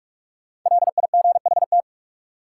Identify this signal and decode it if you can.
Morse code, heard as a keyed tone, about 46 wpm, 700 Hz. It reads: LIGHT
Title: Sentences from Top 200 Words - New Word Drill - 45wpm